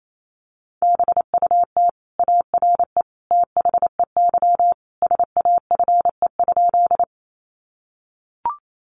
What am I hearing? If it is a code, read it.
Morse code, heard as a keyed tone, about 28 wpm, 690 Hz. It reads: BUT ARE THEY SAFE?
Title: Sentences from Top 900 Words 28wpm